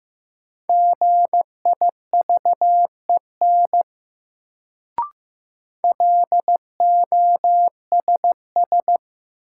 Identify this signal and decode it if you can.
Morse code, heard as a keyed tone, about 15 wpm, 700 Hz. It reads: GIVEN LOSS
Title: Sentences from Top 1400 Words - New Word Drill Alt - 15wpm